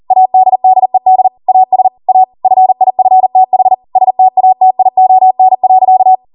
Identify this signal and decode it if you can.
Morse code, heard as a keyed tone, about 40 wpm, 750 Hz. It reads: ADDED AS A FIFTH STATION.